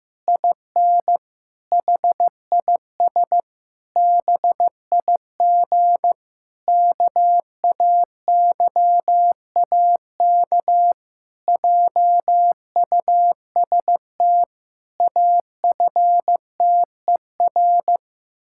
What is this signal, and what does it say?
Morse code, heard as a keyed tone, about 15 wpm, 700 Hz. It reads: IN HIS BIG KAYAK JUST AFTER
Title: Pangram Challenge v2 - 15wpm